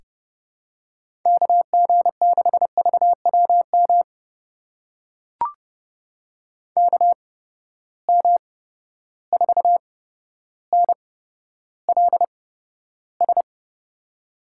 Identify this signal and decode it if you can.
Morse code, heard as a keyed tone, about 30 wpm, spaced as for 13 wpm, 700 Hz. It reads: KG6VWM KM4NLS
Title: USA Call Signs - 6 Characters - 8x Character Spacing - 30wpm